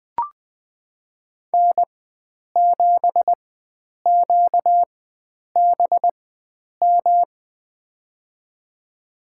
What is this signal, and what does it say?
Morse code, heard as a keyed tone, about 20 wpm, spaced as for 9 wpm, 700 Hz. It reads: N7QBM
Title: USA Call Signs - 5 Characters - 4x Character Spacing - 20wpm